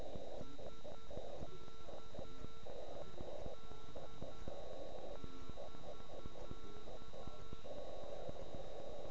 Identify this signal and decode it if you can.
Morse code, heard as a keyed tone, about 9 wpm, 1340 Hz. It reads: SKED <BK>